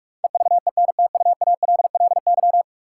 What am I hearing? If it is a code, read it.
Morse code, heard as a keyed tone, about 42 wpm, 700 Hz. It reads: EVENTUALLY
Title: Sentences from Top 1000 Words - Review Part 6 - New Word Drill - 45wpm